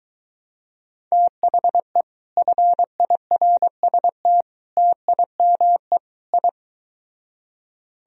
Morse code, heard as a keyed tone, about 22 wpm, 700 Hz. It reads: THE FIRST TIME I